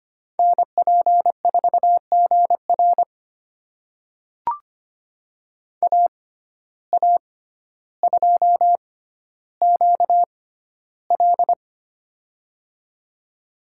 Morse code, heard as a keyed tone, about 25 wpm, spaced as for 13 wpm, 700 Hz. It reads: NP4GR AA2QL
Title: USA Call Signs - 5 Characters - 6x Character Spacing - 25wpm